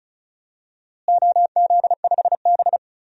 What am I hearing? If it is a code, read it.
Morse code, heard as a keyed tone, about 34 wpm, 700 Hz. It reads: OZ5B
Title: DXCC-100 Call Signs - 4 Characters - Repeated 3x - 35wpm